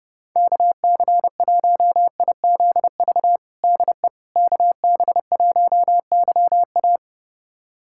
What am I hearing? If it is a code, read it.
Morse code, heard as a keyed tone, about 32 wpm, 700 Hz. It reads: KC1IZV DE KB1YA
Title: USA Callsign DE Callsign 30wpm